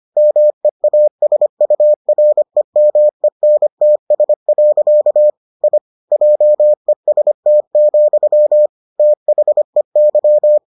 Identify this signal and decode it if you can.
Morse code, heard as a keyed tone, about 25 wpm, 600 Hz. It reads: MEASUREMENTS. I JEST, THEY